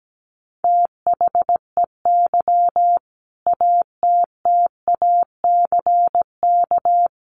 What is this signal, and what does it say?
Morse code, heard as a keyed tone, about 17 wpm, 700 Hz. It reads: THEY ATTACK
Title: Sentences from Top 1500 Words 17wpm